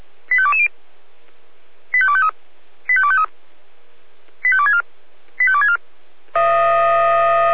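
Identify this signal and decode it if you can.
Five-tone selective call (ZVEI), heard as quick runs of five tones, tones of about 70 ms. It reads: 85220 85253 85253 85264 85264
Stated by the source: Five-tone ZVEI selective call 85250 received by SDR, from audio.f85360.feuerwehr-orlamuende.org fire brigade server.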